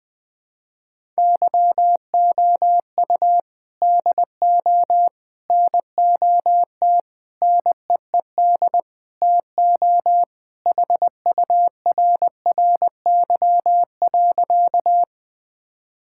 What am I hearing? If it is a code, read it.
Morse code, heard as a keyed tone, about 19 wpm, 700 Hz. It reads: YOU DO NOT NEED TO HURRY.
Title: Sentences with 4-Word Phrases - Spoken Beginning - 20wpm